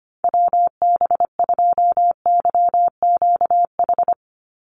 Morse code, heard as a keyed tone, about 25 wpm, 700 Hz. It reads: WB2YQH
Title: Call Signs Repeated 2x - 25wpm